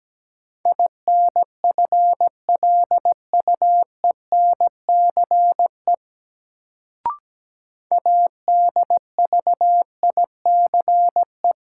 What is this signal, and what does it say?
Morse code, heard as a keyed tone, about 17 wpm, 700 Hz. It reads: INFLUENCE ADVICE
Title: Sentences from Top 1100 Words - New Word Drill - 17wpm